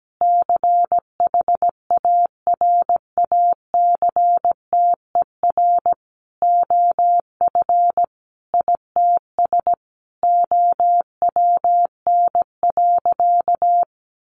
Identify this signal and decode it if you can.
Morse code, heard as a keyed tone, about 17 wpm, 700 Hz. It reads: CHARACTER OF ITS OWN.
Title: Sentences from Top 1200 Words 17wpm